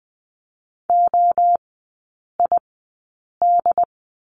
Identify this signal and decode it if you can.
Morse code, heard as a keyed tone, about 20 wpm, spaced as for 8 wpm, 700 Hz. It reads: OID
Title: 067 Lesson 12e - Letter d - Three Characters 20wpm